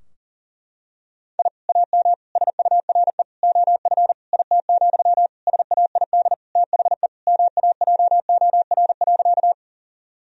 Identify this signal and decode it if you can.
Morse code, heard as a keyed tone, about 40 wpm, 700 Hz. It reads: I AM SURE OF IT, SAID THE MAJOR.
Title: Sentences from Top 1000 Words - Review Part 4 - 40wpm